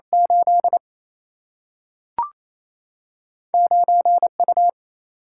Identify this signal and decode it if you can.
Morse code, heard as a keyed tone, about 28 wpm, 700 Hz. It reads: 8 9 U
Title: Single-Letter Single-Number 28wpm